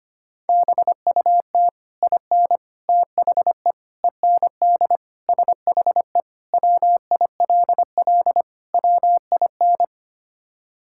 Morse code, heard as a keyed tone, about 25 wpm, 700 Hz. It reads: BUT IN THE END SHE WILL WIN
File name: practice / 25wpm / Sentences with 3-Word Phrases - 25wpm